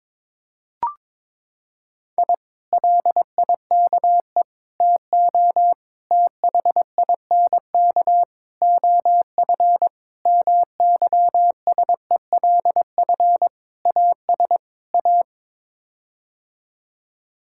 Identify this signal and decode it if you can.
Morse code, heard as a keyed tone, about 22 wpm, 700 Hz. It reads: I LIKE TO THINK OF MYSELF AS A
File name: practice / 22wpm / Sets of 8 Words - 22wpm